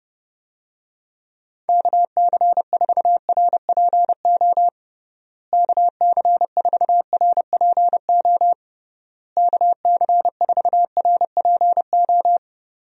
Morse code, heard as a keyed tone, about 30 wpm, 700 Hz. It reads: KC4RPO KC4RPO KC4RPO
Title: USA Call Signs - 6 Characters - Repeated 3x - 30wpm